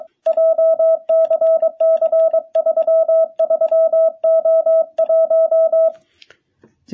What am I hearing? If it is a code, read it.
Morse code, heard as a keyed tone, about 22 wpm, 660 Hz. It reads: JCC33O1
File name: jh8cbh / 2021cw